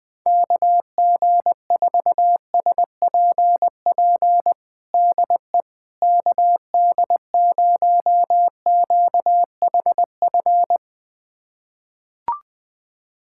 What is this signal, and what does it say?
Morse code, heard as a keyed tone, about 20 wpm, 700 Hz. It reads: KG4SPP DE KD0QHF
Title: USA Callsign DE Callsign - 6 Characters - 20wpm